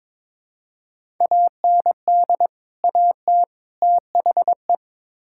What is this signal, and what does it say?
Morse code, heard as a keyed tone, about 22 wpm, 700 Hz. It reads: AND AT THE
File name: practice / 22wpm / Build A Phrase - 5-Word Phrases - 22wpm